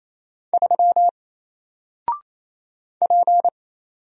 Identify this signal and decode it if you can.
Morse code, heard as a keyed tone, about 27 wpm, 700 Hz. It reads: 3 P
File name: Single-Letter-Number-28wpm